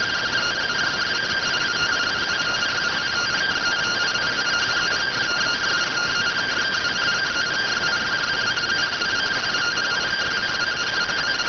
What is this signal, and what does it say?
Two-tone teleprinter signal, about 170 Hz shift.